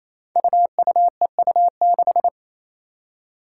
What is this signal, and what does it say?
Morse code, heard as a keyed tone, about 28 wpm, 700 Hz. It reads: UUEU6
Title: Five Challenging Letter-Number Combinations 28wpm